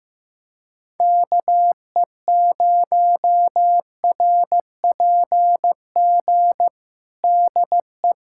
Morse code, heard as a keyed tone, about 15 wpm, 700 Hz. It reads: KE0RPG DE
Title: USA Callsign DE Callsign - 6 Characters - 15wpm